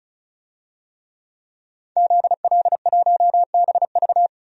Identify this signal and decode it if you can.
Morse code, heard as a keyed tone, about 36 wpm, 700 Hz. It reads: ZL1BV
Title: DXCC-100 Call Signs - 5 Characters - 35wpm